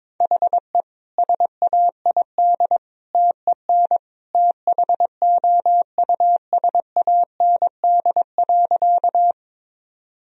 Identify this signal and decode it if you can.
Morse code, heard as a keyed tone, about 22 wpm, 690 Hz. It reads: HE SAID TEN THOUSAND.
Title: Sentences from Top 1000 Words - Review Part 3 - 22wpm